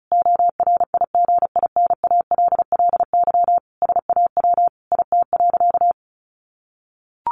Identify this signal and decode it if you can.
Morse code, heard as a keyed tone, about 34 wpm, 700 Hz. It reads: ORIGINALLY SAW IT.
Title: Sentences from Top 1500 Words 35wpm